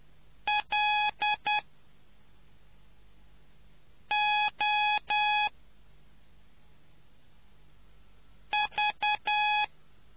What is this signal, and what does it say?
Morse code, heard as a keyed tone, about 10 wpm, 840 Hz. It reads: L O V